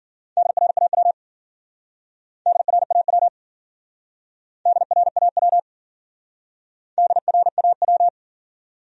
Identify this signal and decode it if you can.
Morse code, heard as a keyed tone, about 52 wpm, 700 Hz. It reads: DRAW DRAW DRAW DRAW